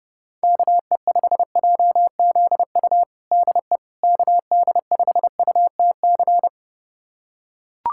Morse code, heard as a keyed tone, about 32 wpm, 700 Hz. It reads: KE5JZU DE KD5UTC
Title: USA Callsign DE Callsign 30wpm